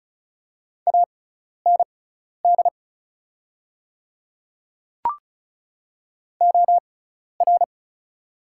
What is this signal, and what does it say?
Morse code, heard as a keyed tone, about 36 wpm, spaced as for 12 wpm, 700 Hz. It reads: AND OR
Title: 6x Character Spacing - Top 300 Words - 35wpm